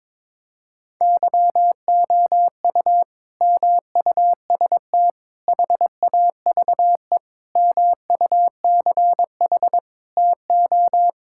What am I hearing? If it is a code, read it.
Morse code, heard as a keyed tone, about 22 wpm, 700 Hz. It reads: YOU MUST HAVE MUCH TO
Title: Sentences from Top 1600 Words 22wpm